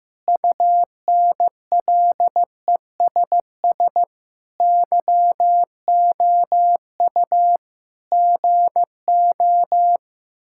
Morse code, heard as a keyed tone, about 15 wpm, 700 Hz. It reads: UNLESS YOU GO
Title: Sentences from Top 1000 Words - Review Part 6 - 15wpm